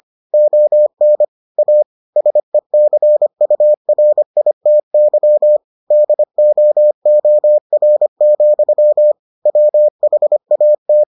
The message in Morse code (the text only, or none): ON A SECURITY DOOR, WHAT